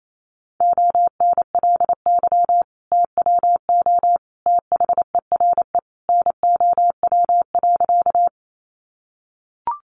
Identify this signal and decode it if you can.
Morse code, heard as a keyed tone, about 28 wpm, 700 Hz. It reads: ONLY TWO THERE NOW.
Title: Sentences from Top 200 Words 28wpm